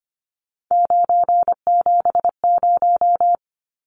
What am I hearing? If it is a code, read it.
Morse code, heard as a keyed tone, about 25 wpm, 700 Hz. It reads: 970
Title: Numbers - 3-Digit 25wpm